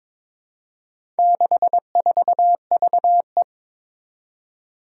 Morse code, heard as a keyed tone, about 22 wpm, 700 Hz. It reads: 64VE
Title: Four Challenging Letter-Number Combinations 22wpm